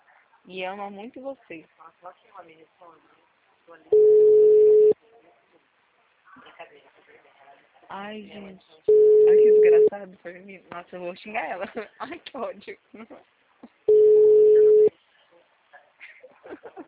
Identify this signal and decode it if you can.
Ringback tone (Europe, 425 Hz), with speech.